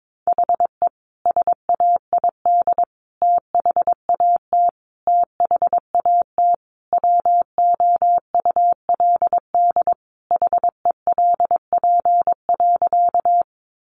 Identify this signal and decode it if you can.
Morse code, heard as a keyed tone, about 22 wpm, 700 Hz. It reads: HE SAID THAT THAT WOULD HELP.
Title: Sentences from Top 200 Words 22wpm